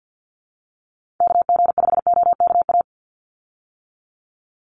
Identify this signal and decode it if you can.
Morse code, heard as a keyed tone, about 50 wpm, 700 Hz. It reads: KG5OKA
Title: USA Call Signs 50wpm